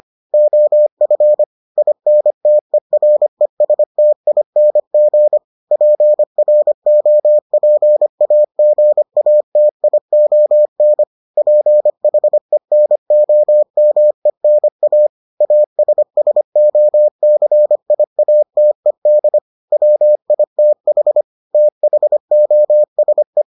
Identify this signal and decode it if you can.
Morse code, heard as a keyed tone, about 25 wpm, 600 Hz. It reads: OF INTERESTING PROPAGATION PHENOMENA ASSOCIATED WITH THOSE